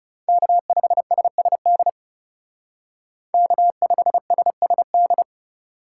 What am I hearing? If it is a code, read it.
Morse code, heard as a keyed tone, about 36 wpm, 700 Hz. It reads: K5SSD K5SSD
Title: Speed Racing - USA 1x3 Call Signs 30wpm